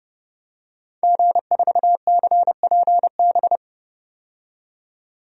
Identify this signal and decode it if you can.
Morse code, heard as a keyed tone, about 32 wpm, 700 Hz. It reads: G4CPB
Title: DXCC-100 Call Signs - 5 Characters - 30wpm